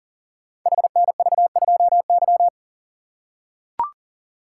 Morse code, heard as a keyed tone, about 41 wpm, 700 Hz. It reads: HNV2Y
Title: Code Groups - 5 Letter:Numbers - 40wpm